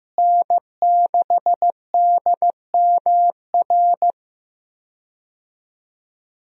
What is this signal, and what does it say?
Morse code, heard as a keyed tone, about 15 wpm, 700 Hz. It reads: N6DMR
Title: USA Call Signs - 5 Characters - Repeated 3x - 15wpm